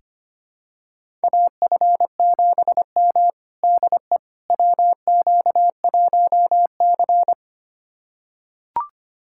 Morse code, heard as a keyed tone, about 25 wpm, 700 Hz. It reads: AF7M DE WQ1C
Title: USA Callsign DE Callsign - 4 Characters - 25wpm